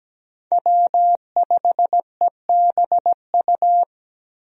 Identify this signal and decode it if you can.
Morse code, heard as a keyed tone, about 17 wpm, 700 Hz. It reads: W5EBU
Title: USA Call Signs - 5 Characters - Repeated 2x - 17wpm